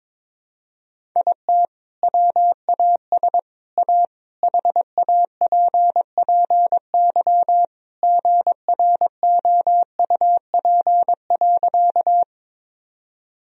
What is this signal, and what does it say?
Morse code, heard as a keyed tone, about 22 wpm, 700 Hz. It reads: IT WAS A HAPPY GROUP.
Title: Sentences from Top 1000 Words - Review Part 1 - 22wpm